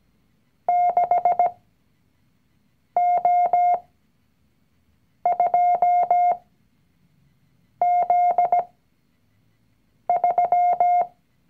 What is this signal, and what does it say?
Morse code, heard as a keyed tone, about 17 wpm, spaced as for 5 wpm, 710 Hz. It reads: 6O2Z3
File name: CW-Online-Kurs / Anfänger / Lektion 019